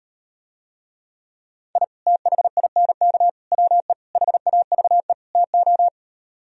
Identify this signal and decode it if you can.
Morse code, heard as a keyed tone, about 38 wpm, 700 Hz. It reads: I THINK WE HAVE TO